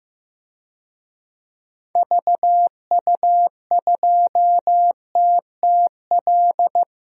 Morse code, heard as a keyed tone, about 15 wpm, 700 Hz. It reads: VU2TTL
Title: DXCC-100 Call Signs - 6 Characters - 15wpm